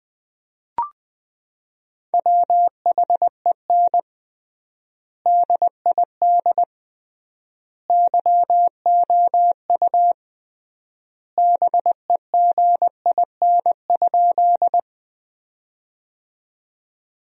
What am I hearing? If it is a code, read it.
Morse code, heard as a keyed tone, about 20 wpm, 700 Hz. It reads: WHEN DID YOU BEGIN?
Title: 3x Word-Spacing - Sentences from Top 300 Words - 20wpm